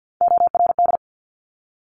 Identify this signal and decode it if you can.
Morse code, heard as a keyed tone, about 50 wpm, 700 Hz. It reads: ORD
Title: Top Three Letter Combinations in Words 50wpm